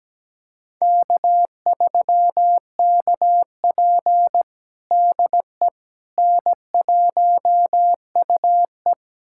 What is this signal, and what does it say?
Morse code, heard as a keyed tone, about 17 wpm, 700 Hz. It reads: K3KP DE N1UE
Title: USA Callsign DE Callsign - 4 Characters - 17wpm